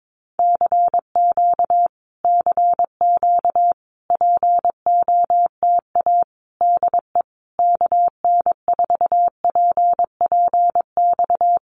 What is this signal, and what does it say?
Morse code, heard as a keyed tone, about 22 wpm, 700 Hz. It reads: CQ CQ POTA DE KN4PPX
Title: Common QSO Phrases Encore 22wpm